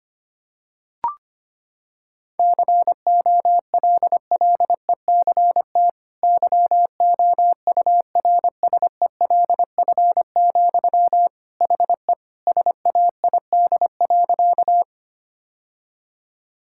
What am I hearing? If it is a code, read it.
Morse code, heard as a keyed tone, about 25 wpm, 700 Hz. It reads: COLLECT YOURSELF, HE SAID.
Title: Sentences from Top 1600 Words 25wpm